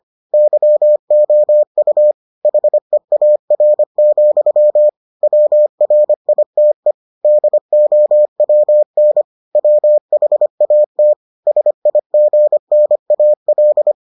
Morse code, heard as a keyed tone, about 25 wpm, 600 Hz. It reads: YOU HEAR, WRITE DOWN WHAT SIGNAL